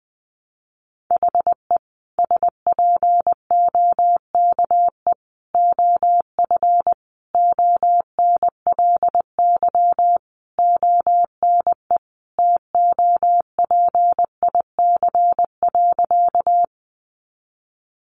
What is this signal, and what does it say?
Morse code, heard as a keyed tone, about 20 wpm, 700 Hz. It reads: HE SPOKE OF ONLY ONE TOPIC.
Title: Sentences from Top 1200 Words 20wpm